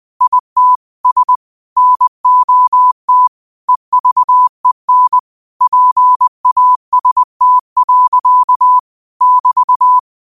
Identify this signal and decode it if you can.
Morse code, heard as a keyed tone, about 20 wpm, 1000 Hz. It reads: IT S NOT EVEN PAST. <BT>
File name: qotdfiles / 20wpm